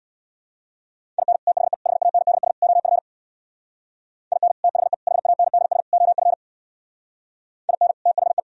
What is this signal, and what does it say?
Morse code, heard as a keyed tone, about 64 wpm, 690 Hz. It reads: IN THE HANDS OF IN THE HANDS OF IN THE